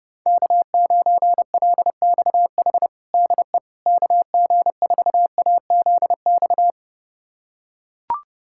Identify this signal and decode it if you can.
Morse code, heard as a keyed tone, about 32 wpm, 700 Hz. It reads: K9LXH DE KG4AZX
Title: USA Callsign DE Callsign 30wpm